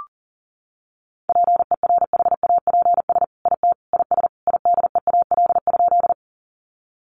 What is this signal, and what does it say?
Morse code, heard as a keyed tone, about 40 wpm, 700 Hz. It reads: PERHAPS IT IS IDEAL?
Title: Sentences from Top 1400 Words 40wpm